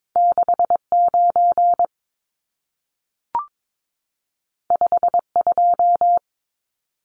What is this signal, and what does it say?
Morse code, heard as a keyed tone, about 22 wpm, 700 Hz. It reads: 69 52
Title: Numbers - 2-Digit 22wpm